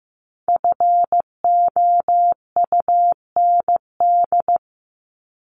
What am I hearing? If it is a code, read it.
Morse code, heard as a keyed tone, about 15 wpm, 700 Hz. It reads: FOUND
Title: Top 300 Words Repeated 15wpm